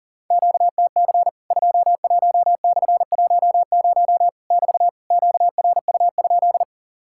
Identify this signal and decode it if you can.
Morse code, heard as a keyed tone, about 40 wpm, 700 Hz. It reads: QTC 21/10 <BT> QRU?
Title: Worked All Europe DX Contest - QTC Exchange - 40wpm